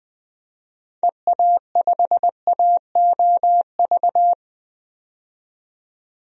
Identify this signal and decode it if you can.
Morse code, heard as a keyed tone, about 21 wpm, 700 Hz. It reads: EA5AOV
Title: DXCC-100 Call Signs - 6 Characters - 20wpm